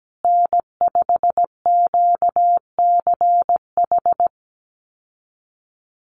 Morse code, heard as a keyed tone, about 17 wpm, 700 Hz. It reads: N5QCH
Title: USA Call Signs Repeated 3x 17wpm